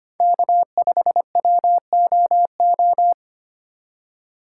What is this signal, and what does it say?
Morse code, heard as a keyed tone, about 31 wpm, 700 Hz. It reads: K5WOO
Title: Speed Racing - USA 1x3 Call Signs 25wpm